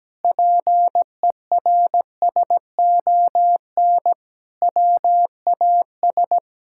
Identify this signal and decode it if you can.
Morse code, heard as a keyed tone, about 17 wpm, 700 Hz. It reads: PERSON WAS
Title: Sentences from Top 1000 Words - Review Part 5 - 17wpm